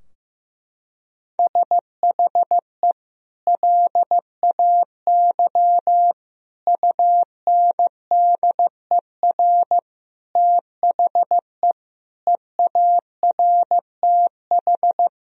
Morse code, heard as a keyed tone, about 15 wpm, 700 Hz. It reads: SHE LAY UNDER THE EARTH
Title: Sentences from Top 1000 Words - Review Part 2 - 15wpm